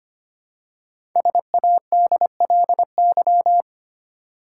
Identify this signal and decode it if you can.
Morse code, heard as a keyed tone, about 25 wpm, 700 Hz. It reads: SADLY
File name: 5 Letter Words - First 3 Letters Given - 25wpm